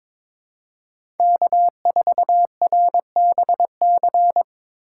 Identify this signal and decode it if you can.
Morse code, heard as a keyed tone, about 25 wpm, 700 Hz. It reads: K4RBC
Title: Speed Racing - USA Call Signs 22wpm